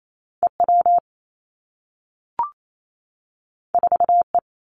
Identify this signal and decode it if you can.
Morse code, heard as a keyed tone, about 28 wpm, 700 Hz. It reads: E W 4 E
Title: Two Challenging Letter-Number Combinations 28wpm